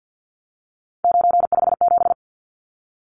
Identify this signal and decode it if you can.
Morse code, heard as a keyed tone, about 49 wpm, 700 Hz. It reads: 957
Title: Numbers - 3-Digit 50wpm